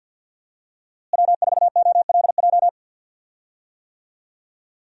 Morse code, heard as a keyed tone, about 51 wpm, 700 Hz. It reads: W4OLJ
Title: USA Call Signs - 5 Characters - Repeated 2x - 50wpm